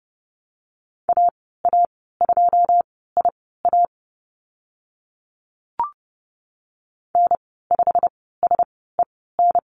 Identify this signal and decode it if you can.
Morse code, heard as a keyed tone, about 30 wpm, spaced as for 17 wpm, 700 Hz. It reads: AA2IA N5SEN
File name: Lesson 15g - Number 5 - Call Signs 30wpm